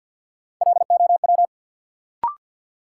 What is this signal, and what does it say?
Morse code, heard as a keyed tone, about 52 wpm, 700 Hz. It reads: LOW
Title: Sentences from Top 500 Words - New Word Drill - 50wpm